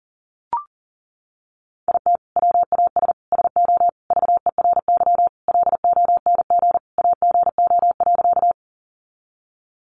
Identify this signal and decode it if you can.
Morse code, heard as a keyed tone, about 40 wpm, 700 Hz. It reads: IT WAS SO VERY LONG AGO.
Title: Sentences from Top 500 Words 40wpm